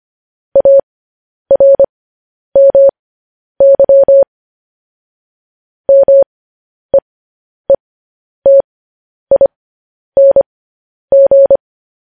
Morse code, heard as a keyed tone, about 25 wpm, spaced as for 10 wpm, 550 Hz. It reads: ARMY MEETING